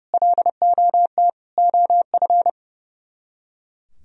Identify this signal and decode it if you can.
Morse code, heard as a keyed tone, about 37 wpm, 700 Hz. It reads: LOT OF
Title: Speed-Racing - Sets of 5 Words 30wpm